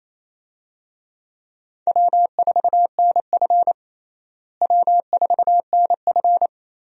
Morse code, heard as a keyed tone, about 28 wpm, 700 Hz. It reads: W4NF W4NF
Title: USA Call Signs - 4 Characters - Repeated 2x - 28wpm